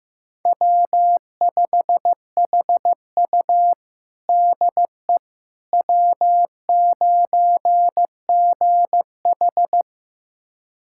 Morse code, heard as a keyed tone, about 15 wpm, 700 Hz. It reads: W5HU DE W9GH
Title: USA Callsign DE Callsign - 4 Characters - 15wpm